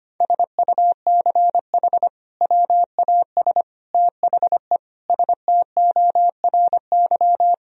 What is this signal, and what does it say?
Morse code, heard as a keyed tone, about 25 wpm, 700 Hz. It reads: SUCH WAS THE STORY
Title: Sentences from Top 1000 Words - Review Part 3 - 25wpm